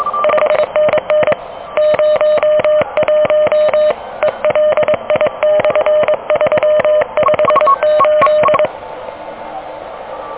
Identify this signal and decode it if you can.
Morse code, heard as a keyed tone, about 22 wpm, 600 Hz. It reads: 5NN 01 ELI/358
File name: OK QRP 2023 / OK QRP 2023